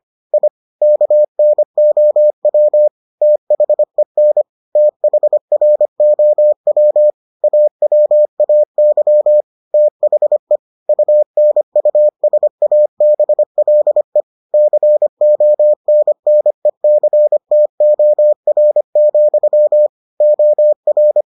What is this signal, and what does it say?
Morse code, heard as a keyed tone, about 25 wpm, 600 Hz. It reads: I KNOW THEN THROW AWAY THE UNUSABLE CONNECTOR, OR